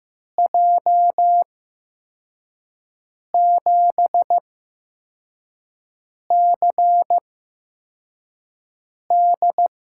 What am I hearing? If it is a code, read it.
Morse code, heard as a keyed tone, about 15 wpm, spaced as for 4 wpm, 700 Hz. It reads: J7CD
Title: USA Call Signs - 6 Characters - 8x Character Spacing - 15wpm